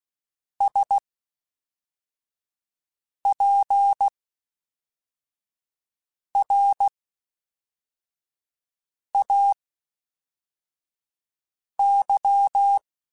Morse code, heard as a keyed tone, about 16 wpm, spaced as for 4 wpm, 800 Hz. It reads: SPRAY
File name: Special INGs / Special INGs-mp3s / Fast ing @10